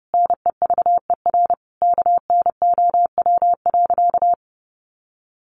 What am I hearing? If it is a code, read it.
Morse code, heard as a keyed tone, about 30 wpm, 700 Hz. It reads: NEVER KNOW.
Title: Sentences from Top 300 Words 30wpm